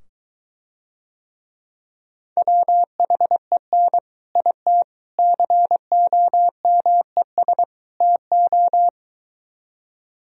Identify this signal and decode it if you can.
Morse code, heard as a keyed tone, about 23 wpm, 700 Hz. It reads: WHEN IT COMES TO